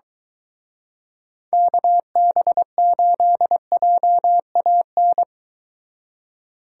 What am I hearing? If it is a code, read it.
Morse code, heard as a keyed tone, about 23 wpm, 700 Hz. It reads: KB8JAN